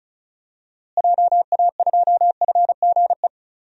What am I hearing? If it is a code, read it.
Morse code, heard as a keyed tone, about 36 wpm, 700 Hz. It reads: JA2FGE
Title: DXCC-100 Call Signs - 6 Characters - Repeated 2x - 35wpm